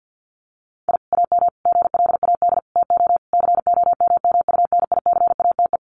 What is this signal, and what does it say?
Morse code, heard as a keyed tone, about 50 wpm, 700 Hz. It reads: I AM GLAD TO COMMUNICATE
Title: Sentences from Top 1300 Words 50wpm